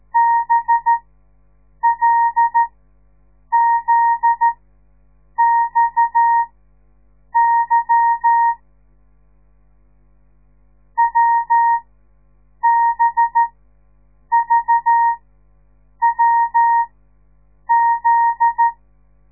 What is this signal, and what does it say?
Morse code, heard as a keyed tone, about 13 wpm, spaced as for 7 wpm, 930 Hz. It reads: BLZXY WBVWZ
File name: radiomorningstar / morse / les 9